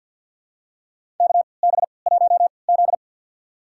Morse code, heard as a keyed tone, about 52 wpm, spaced as for 28 wpm, 700 Hz. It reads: KB1Z
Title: USA Call Signs - 3x Character Spacing - 50wpm